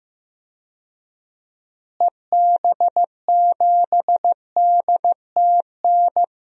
Morse code, heard as a keyed tone, about 16 wpm, 700 Hz. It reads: EB7DTN
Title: DXCC-100 Call Signs - 6 Characters - Repeated 2x - 15wpm